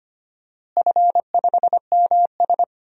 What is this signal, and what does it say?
Morse code, heard as a keyed tone, about 25 wpm, 700 Hz. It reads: F5MS
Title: DXCC-100 Call Signs - 4 Characters - Repeated 3x - 25wpm